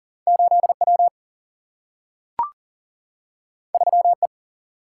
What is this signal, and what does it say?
Morse code, heard as a keyed tone, about 40 wpm, 700 Hz. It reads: 8 W 3 E
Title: Single-Letter Single-Number 40wpm